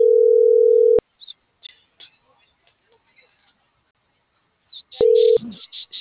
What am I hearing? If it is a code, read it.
Ringback tone (North America), 440 + 480 Hz.
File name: CENTRAL TERRESTRE / out lucero.sky